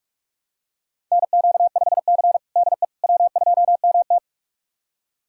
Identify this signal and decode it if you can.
Morse code, heard as a keyed tone, about 42 wpm, 700 Hz. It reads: NQ5C DE W2MT
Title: USA Callsign DE Callsign - 4 Characters - 45wpm